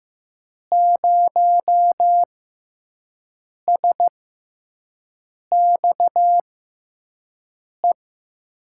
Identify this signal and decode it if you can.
Morse code, heard as a keyed tone, about 15 wpm, spaced as for 5 wpm, 700 Hz. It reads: 0SXE
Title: USA Call Signs - 6 Characters - 6x Character Spacing - 15wpm